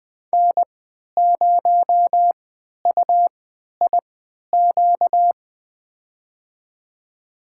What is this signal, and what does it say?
Morse code, heard as a keyed tone, about 20 wpm, spaced as for 11 wpm, 700 Hz. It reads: N0UIQ
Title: USA Call Signs - 5 Characters - 3x Character Spacing - 20wpm